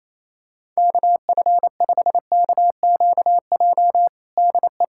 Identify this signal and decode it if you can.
Morse code, heard as a keyed tone, about 28 wpm, 700 Hz. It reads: KF5KQJ DE
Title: USA Callsign DE Callsign 28wpm